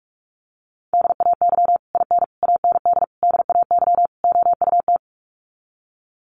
Morse code, heard as a keyed tone, about 44 wpm, 700 Hz. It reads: DAY IN AND DAY OUT
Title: Sets of 5 Words Encore2 45wpm